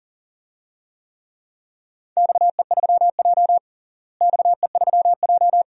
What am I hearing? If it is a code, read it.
Morse code, heard as a keyed tone, about 40 wpm, 690 Hz. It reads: XE3J XE3J
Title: DXCC-100 Call Signs - 4 Characters - Repeated 2x - 40wpm